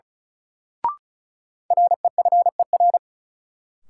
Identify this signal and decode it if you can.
Morse code, heard as a keyed tone, about 36 wpm, 700 Hz. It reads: REFER
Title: Sentences from Top 1000 Words - New Word Drill - 35wpm